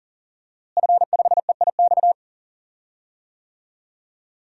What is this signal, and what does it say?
Morse code, heard as a keyed tone, about 40 wpm, 690 Hz. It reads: F5EIX
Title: DXCC-100 Call Signs - 5 Characters - Repeated 2x - 40wpm